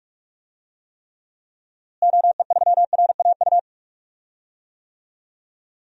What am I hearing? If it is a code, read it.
Morse code, heard as a keyed tone, about 46 wpm, 700 Hz. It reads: OE3RAU
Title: DXCC-100 Call Signs - 6 Characters - Repeated 3x - 45wpm